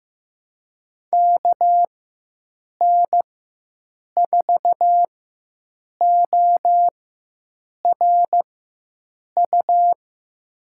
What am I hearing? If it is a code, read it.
Morse code, heard as a keyed tone, about 15 wpm, spaced as for 7 wpm, 700 Hz. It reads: KN4ORU
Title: USA Call Signs - 4x Character Spacing - 15wpm